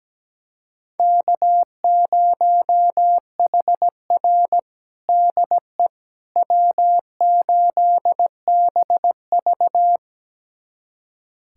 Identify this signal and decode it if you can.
Morse code, heard as a keyed tone, about 17 wpm, 700 Hz. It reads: K0HR DE W8BV
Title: USA Callsign DE Callsign - 4 Characters - 17wpm